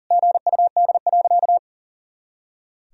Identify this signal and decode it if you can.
Morse code, heard as a keyed tone, about 40 wpm, 700 Hz. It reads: GUD.
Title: Sentences from Top 500 Words and Abbreviations 40wpm